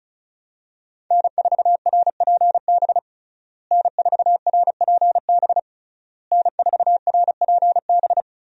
Morse code, heard as a keyed tone, about 34 wpm, 700 Hz. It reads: N4RPB N4RPB N4RPB
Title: USA Call Signs - 5 Characters - Repeated 3x - 35wpm